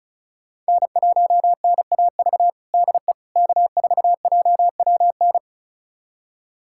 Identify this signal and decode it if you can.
Morse code, heard as a keyed tone, about 36 wpm, 700 Hz. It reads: N1NAV DE K4JWN
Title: USA Callsign DE Callsign - 5 Characters - 35wpm